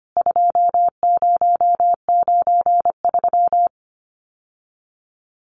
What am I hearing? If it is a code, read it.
Morse code, heard as a keyed tone, about 25 wpm, 700 Hz. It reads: 2093
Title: CWT Exchange 25wpm